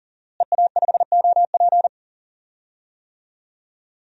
Morse code, heard as a keyed tone, about 37 wpm, 700 Hz. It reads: EA5OP
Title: DXCC-100 Call Signs - 5 Characters - Repeated 2x - 40wpm